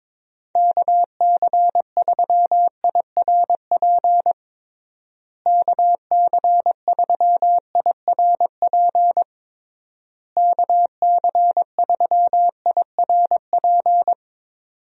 Morse code, heard as a keyed tone, about 21 wpm, 700 Hz. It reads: KC3IRP KC3IRP KC3IRP
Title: USA Call Signs - 6 Characters - Repeated 3x - 22wpm